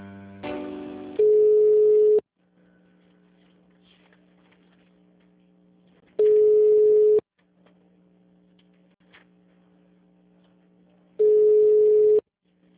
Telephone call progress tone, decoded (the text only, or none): ringback tone